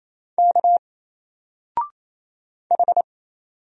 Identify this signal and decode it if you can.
Morse code, heard as a keyed tone, about 27 wpm, 700 Hz. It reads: K H
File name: Single-Letter-Number-28wpm